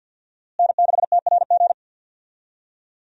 Morse code, heard as a keyed tone, about 53 wpm, 700 Hz. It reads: N6TRG
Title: USA Call Signs - 5 Characters - Repeated 3x - 50wpm